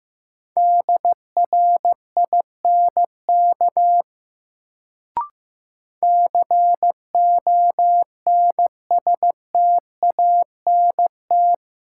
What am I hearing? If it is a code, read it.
Morse code, heard as a keyed tone, about 15 wpm, 700 Hz. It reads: DRINK CONSTANT
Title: Sentences from Top 1500 Words - New Word Drill Alt - 15wpm